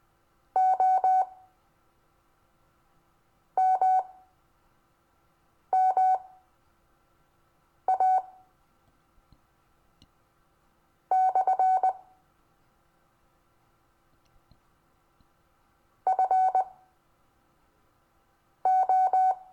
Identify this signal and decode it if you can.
Morse code, heard as a keyed tone, about 19 wpm, spaced as for 4 wpm, 750 Hz. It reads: O MMA / FO
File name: cw / Lesson26 SUMMARY